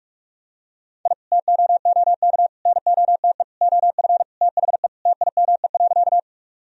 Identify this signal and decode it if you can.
Morse code, heard as a keyed tone, about 46 wpm, 700 Hz. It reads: I TOOK NOTE OF THE TIME.
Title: Sentences from Top 1000 Words - Review Part 2 - 45wpm